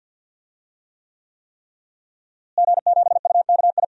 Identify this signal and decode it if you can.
Morse code, heard as a keyed tone, about 48 wpm, 700 Hz. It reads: G7UKI
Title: DXCC-100 Call Signs - 5 Characters - Repeated 3x - 50wpm